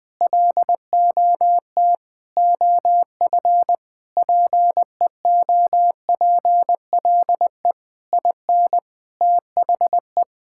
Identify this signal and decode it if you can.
Morse code, heard as a keyed tone, about 20 wpm, 700 Hz. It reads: LOT OF PEOPLE IN THE
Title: Sentences with 5-Word Phrases - 20wpm